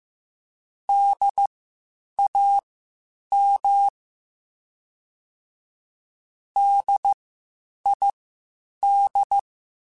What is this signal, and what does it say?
Morse code, heard as a keyed tone, about 15 wpm, spaced as for 8 wpm, 800 Hz. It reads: DAM DID